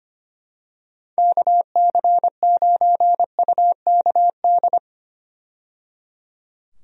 Morse code, heard as a keyed tone, about 25 wpm, 700 Hz. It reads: KC9UKD
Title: USA Call Signs - 6 Characters - 25wpm